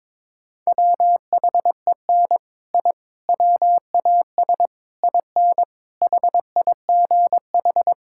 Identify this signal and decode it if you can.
Morse code, heard as a keyed tone, about 22 wpm, 700 Hz. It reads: WHEN I WAS IN HIGH
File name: practice / 22wpm / Sets of 6 Words - 22wpm